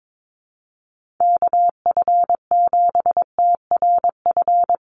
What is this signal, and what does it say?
Morse code, heard as a keyed tone, about 22 wpm, 700 Hz. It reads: KF7TRF
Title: USA Call Signs Repeated 2x 22wpm